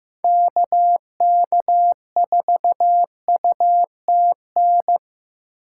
Morse code, heard as a keyed tone, about 15 wpm, 700 Hz. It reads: KK4UTN
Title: USA Call Signs - 2x Character Spacing - 15wpm